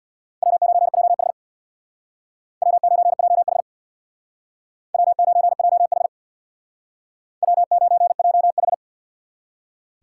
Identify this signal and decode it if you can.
Morse code, heard as a keyed tone, about 65 wpm, 700 Hz. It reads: W9JH W9JH W9JH W9JH